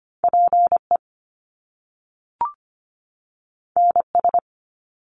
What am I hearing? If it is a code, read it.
Morse code, heard as a keyed tone, about 25 wpm, 700 Hz. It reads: PE NS
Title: Canadian Provinces and Territories 25wpm